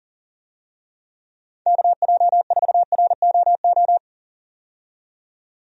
Morse code, heard as a keyed tone, about 40 wpm, 700 Hz. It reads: KJ4ROO
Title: USA Call Signs - 6 Characters - 40wpm